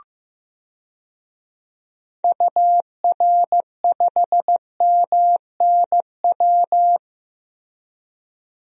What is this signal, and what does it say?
Morse code, heard as a keyed tone, about 15 wpm, 700 Hz. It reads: UR5MNW
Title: DXCC-100 Call Signs - 6 Characters - 15wpm